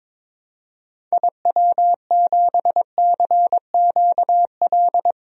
Morse code, heard as a keyed tone, about 22 wpm, 700 Hz. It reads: IW7CQL
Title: DXCC-100 Call Signs - 6 Characters - 22wpm